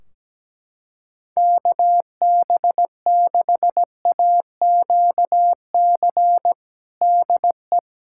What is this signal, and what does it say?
Morse code, heard as a keyed tone, about 17 wpm, 700 Hz. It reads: KB6AQC DE
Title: K1USN SST QSO Practice 17wpm